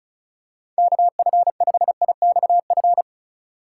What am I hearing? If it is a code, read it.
Morse code, heard as a keyed tone, about 36 wpm, 700 Hz. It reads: KF5IXF
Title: USA Call Signs - 6 Characters - 2x Character Spacing - 35wpm